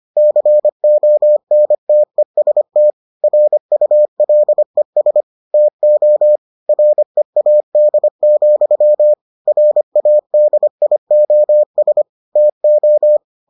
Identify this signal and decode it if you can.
Morse code, heard as a keyed tone, about 25 wpm, 600 Hz. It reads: CONTEST RULES TO READ, RADIOS TO